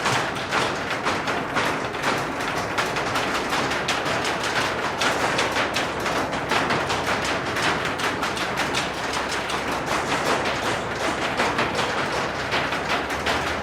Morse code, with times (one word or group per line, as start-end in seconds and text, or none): none